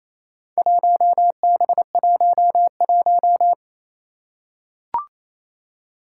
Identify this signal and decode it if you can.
Morse code, heard as a keyed tone, about 28 wpm, 700 Hz. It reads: 1B11
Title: Four Challenging Letter-Number Combinations 28wpm